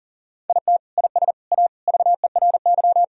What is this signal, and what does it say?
Morse code, heard as a keyed tone, about 40 wpm, 700 Hz. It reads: IT IS A VERY
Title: Sentences from Top 1000 Words - Review Part 4 - 40wpm